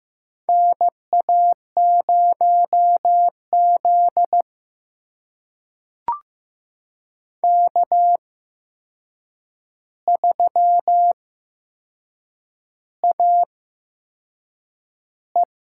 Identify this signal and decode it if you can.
Morse code, heard as a keyed tone, about 15 wpm, spaced as for 6 wpm, 700 Hz. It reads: NA0Z K3AE
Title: USA Call Signs - 4 Characters - 8x Character Spacing - 15wpm